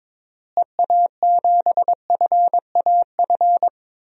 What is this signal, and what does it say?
Morse code, heard as a keyed tone, about 22 wpm, 700 Hz. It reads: EA7FAF
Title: DXCC-100 Call Signs - 6 Characters - Repeated 3x - 22wpm